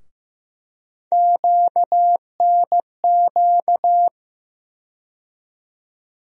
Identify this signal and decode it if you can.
Morse code, heard as a keyed tone, about 15 wpm, 700 Hz. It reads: QNQ
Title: USA Call Sign Postfixes - 3 Letters 15wpm